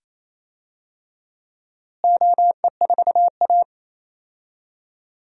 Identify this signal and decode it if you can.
Morse code, heard as a keyed tone, about 28 wpm, 700 Hz. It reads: OE4A
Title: DXCC-100 Call Signs - 4 Characters - Repeated 2x - 28wpm